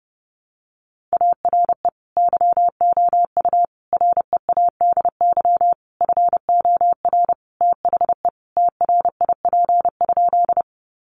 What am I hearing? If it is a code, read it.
Morse code, heard as a keyed tone, about 30 wpm, 700 Hz. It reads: ARE YOU READY FOR THE TRIP?
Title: Sentences from Top 1300 Words 30wpm